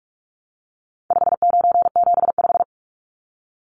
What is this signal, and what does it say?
Morse code, heard as a keyed tone, about 44 wpm, 700 Hz. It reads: 5975
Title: Numbers - 4-Digit 45wpm